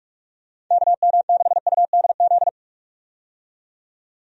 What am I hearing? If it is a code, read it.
Morse code, heard as a keyed tone, about 46 wpm, 700 Hz. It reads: KM6UDZ
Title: USA Call Signs - 6 Characters - 45wpm